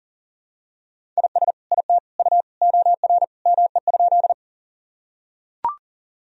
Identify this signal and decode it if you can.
Morse code, heard as a keyed tone, about 40 wpm, 700 Hz. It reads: IS IT U OR ME?
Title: Sentences from Top 200 Words and Abbreviations 40wpm